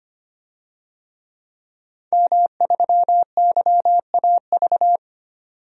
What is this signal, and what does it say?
Morse code, heard as a keyed tone, about 25 wpm, 700 Hz. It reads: M3YAV
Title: DXCC-100 Call Signs - 5 Characters - 25wpm